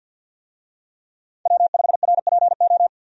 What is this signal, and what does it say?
Morse code, heard as a keyed tone, about 52 wpm, 700 Hz. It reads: W5RPO
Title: USA Call Signs - 5 Characters - 50wpm